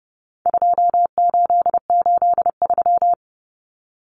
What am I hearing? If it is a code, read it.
Morse code, heard as a keyed tone, about 30 wpm, 700 Hz. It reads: 2883
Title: Common Cut Numbers - 4-Digit 30wpm